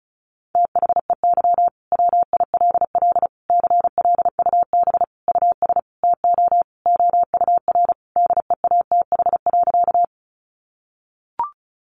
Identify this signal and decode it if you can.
Morse code, heard as a keyed tone, about 34 wpm, 700 Hz. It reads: THEY WILL CLUB US TO OUR DEATH.
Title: Sentences from Top 1200 Words 35wpm